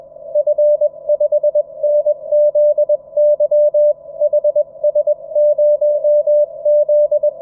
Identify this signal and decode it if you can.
Morse code, heard as a keyed tone, about 21 wpm, 590 Hz. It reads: F5NZY HS0Z
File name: F5NZY-QRP-5W-HS0ZME-20M-06-01-2019